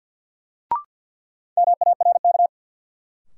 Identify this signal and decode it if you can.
Morse code, heard as a keyed tone, about 53 wpm, 700 Hz. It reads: MARK
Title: Sentences from Top 300 Words - New Word Drill - 50wpm